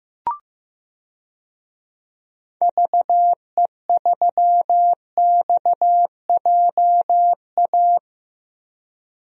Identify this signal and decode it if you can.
Morse code, heard as a keyed tone, about 15 wpm, 700 Hz. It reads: VE3XJA
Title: DXCC-100 Call Signs - 6 Characters - Repeated 2x - 15wpm